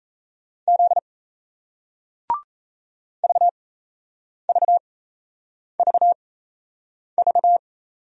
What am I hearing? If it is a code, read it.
Morse code, heard as a keyed tone, about 38 wpm, spaced as for 8 wpm, 700 Hz. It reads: Z VVVV